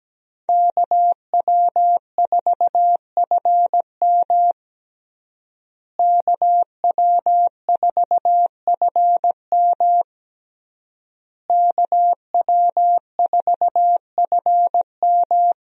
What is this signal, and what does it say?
Morse code, heard as a keyed tone, about 17 wpm, 700 Hz. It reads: KW4FM KW4FM KW4FM
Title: USA Call Signs - 5 Characters - Repeated 3x - 17wpm